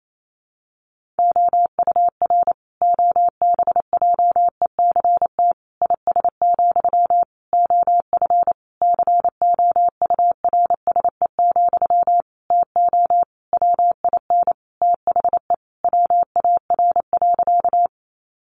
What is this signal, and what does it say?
Morse code, heard as a keyed tone, about 28 wpm, 700 Hz. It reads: OUR OBJECT IS, OF COURSE, TO WIN THE WAR.
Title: Sentences from Top 1500 Words 28wpm